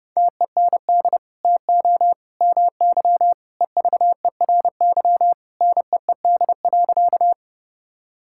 Morse code, heard as a keyed tone, about 30 wpm, 700 Hz. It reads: TEND TO MY EVERY NEED.
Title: Sentences from Top 1000 Words - Review Part 5 - 30wpm